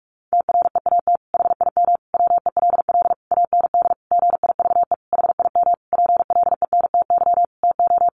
Morse code, heard as a keyed tone, about 44 wpm, 700 Hz. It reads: TREAT HIM WELL AND GIVE HIM PLENTY TO
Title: Sentences from Top 1300 Words 45wpm